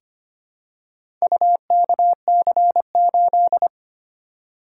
Morse code, heard as a keyed tone, about 25 wpm, 700 Hz. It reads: UKC8
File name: practice / 25wpm / Four Letter-Number Combinations - 25wpm